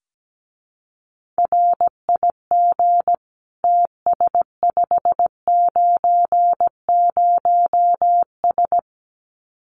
Morse code, heard as a keyed tone, about 17 wpm, 700 Hz. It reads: RIG TS590S
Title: Common QSO Phrases Encore 17wpm